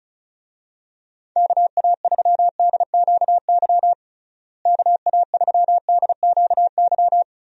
Morse code, heard as a keyed tone, about 34 wpm, 700 Hz. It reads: KA3DQY KA3DQY
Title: USA Call Signs - 6 Characters - Repeated 3x - 35wpm